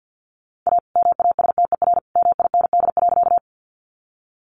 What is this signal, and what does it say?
Morse code, heard as a keyed tone, about 50 wpm, 700 Hz. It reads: A MASTER MIND.
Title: Sentences from Top 1300 Words 50wpm